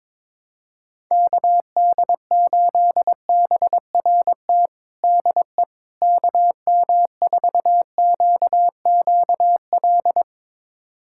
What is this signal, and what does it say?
Morse code, heard as a keyed tone, about 22 wpm, 700 Hz. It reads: KD8BRT DE KM4QQL
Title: USA Callsign DE Callsign 22wpm